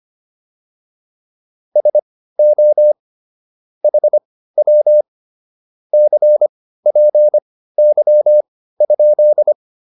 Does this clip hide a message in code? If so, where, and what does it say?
Yes: Morse code from 1.8 to 9.5 s, heard as a keyed tone, about 25 wpm, spaced as for 15 wpm, 600 Hz. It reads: SO HW CPY?